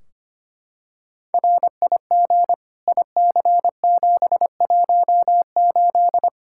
Morse code, heard as a keyed tone, about 25 wpm, 700 Hz. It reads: RIG IC718
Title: Practice QSO - Exchange 2 - Info - 25wpm